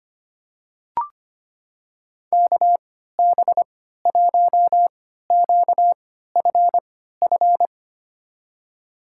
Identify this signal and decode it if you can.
Morse code, heard as a keyed tone, about 25 wpm, spaced as for 14 wpm, 700 Hz. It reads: KB1QFF
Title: USA Call Signs - 6 Characters - 3x Character Spacing - 25wpm